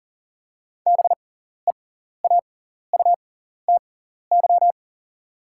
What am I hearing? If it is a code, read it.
Morse code, heard as a keyed tone, about 40 wpm, spaced as for 14 wpm, 690 Hz. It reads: BEAUTY
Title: 6x Character Spacing - Top 500 Words - 40wpm